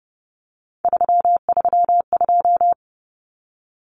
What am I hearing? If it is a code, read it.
Morse code, heard as a keyed tone, about 30 wpm, 700 Hz. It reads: 332
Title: Numbers - 3-Digit 30wpm